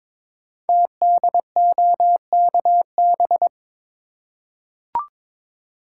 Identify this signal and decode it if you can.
Morse code, heard as a keyed tone, about 22 wpm, 700 Hz. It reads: TDOKB
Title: Code Groups - 5 Letters - 22wpm